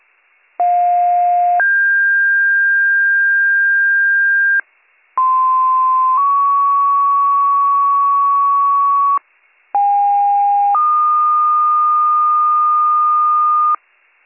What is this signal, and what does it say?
Two-tone page, heard as a one-second tone, then a three-second tone, about 710 then 1600 Hz, about 1040 then 1120 Hz, about 810 then 1220 Hz. It is heard from 0.6 to 13.8 s.